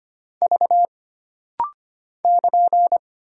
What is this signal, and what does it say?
Morse code, heard as a keyed tone, about 25 wpm, 700 Hz. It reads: V <KN>
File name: Single Character (Letter:Number:Punctuation:Prosign) - Rapid-Fire - 25wpm